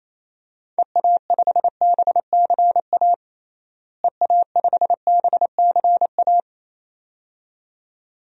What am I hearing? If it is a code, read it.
Morse code, heard as a keyed tone, about 28 wpm, 700 Hz. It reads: EA5BCA EA5BCA
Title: DXCC-100 Call Signs - 6 Characters - Repeated 2x - 28wpm